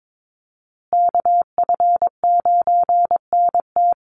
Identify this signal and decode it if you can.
Morse code, heard as a keyed tone, about 22 wpm, 700 Hz. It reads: KF9NT
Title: USA Call Signs Repeated 2x 22wpm